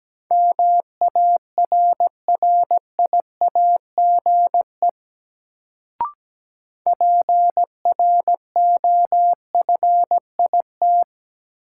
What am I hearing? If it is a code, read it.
Morse code, heard as a keyed tone, about 17 wpm, 700 Hz. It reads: MARRIAGE PROFIT
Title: Sentences from Top 800 Words - New Word Drill - 17wpm